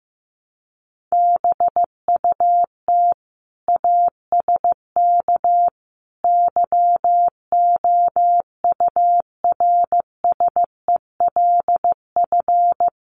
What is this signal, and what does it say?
Morse code, heard as a keyed tone, about 15 wpm, 700 Hz. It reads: BUT ASK YOURSELF
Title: Sentences from Top 600 Words 15wpm